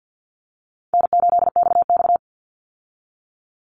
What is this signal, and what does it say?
Morse code, heard as a keyed tone, about 50 wpm, 700 Hz. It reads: N8XX
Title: Easy Call Signs - 50wpm